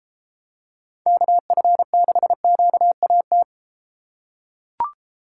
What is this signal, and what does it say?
Morse code, heard as a keyed tone, about 33 wpm, 700 Hz. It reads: KF6QAT